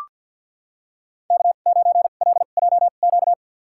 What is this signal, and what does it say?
Morse code, heard as a keyed tone, about 45 wpm, spaced as for 34 wpm, 700 Hz. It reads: K9LJQ
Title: USA Call Signs - 5 Characters - 2x Character Spacing - 50wpm